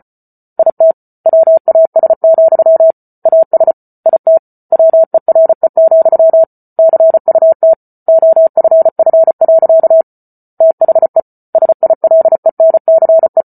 Morse code, heard as a keyed tone, about 34 wpm, 650 Hz. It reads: IT WAS, AS IT WERE, CUT OFF. THE SILENCE